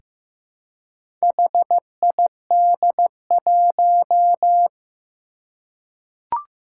Morse code, heard as a keyed tone, about 15 wpm, 700 Hz. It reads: HID1
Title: Four Challenging Letter-Number Combinations 15wpm